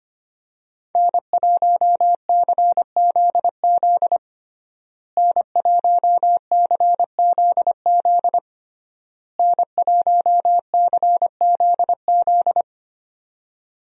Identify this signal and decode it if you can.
Morse code, heard as a keyed tone, about 25 wpm, 700 Hz. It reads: N1CZZ N1CZZ N1CZZ
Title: USA Call Signs - 5 Characters - Repeated 3x - 25wpm